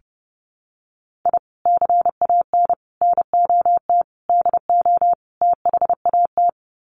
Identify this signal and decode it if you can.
Morse code, heard as a keyed tone, about 28 wpm, 700 Hz. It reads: I CAN NOT DO THAT
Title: Sets of 5 Words Encore2 30wpm